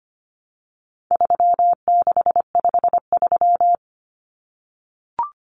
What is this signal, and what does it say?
Morse code, heard as a keyed tone, about 25 wpm, 700 Hz. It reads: 3653
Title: Common Cut Numbers - 4-Digit 25wpm